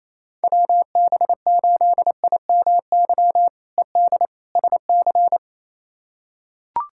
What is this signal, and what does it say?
Morse code, heard as a keyed tone, about 28 wpm, 700 Hz. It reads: WB8IMY ED SC
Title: K1USN SST QSO Practice - 28wpm